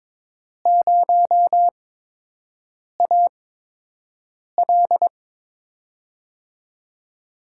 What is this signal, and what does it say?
Morse code, heard as a keyed tone, about 21 wpm, spaced as for 6 wpm, 700 Hz. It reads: 0AL
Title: USA Call Signs - 5 Characters - 8x Character Spacing - 22wpm